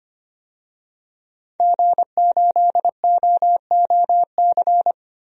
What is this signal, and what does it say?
Morse code, heard as a keyed tone, about 25 wpm, 700 Hz. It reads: G8OOC
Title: DXCC-100 Call Signs - 5 Characters - Repeated 3x - 25wpm